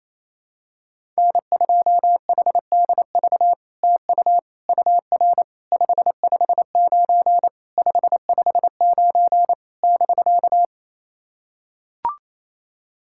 Morse code, heard as a keyed tone, about 28 wpm, 700 Hz. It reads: N2HDV TU UR 559 559 <BK>
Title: SOTA QSO Practice 28wpm